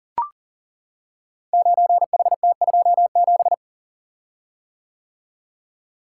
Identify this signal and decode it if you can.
Morse code, heard as a keyed tone, about 41 wpm, 700 Hz. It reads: 9HT27
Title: Code Groups - 5 Letter:Numbers - 40wpm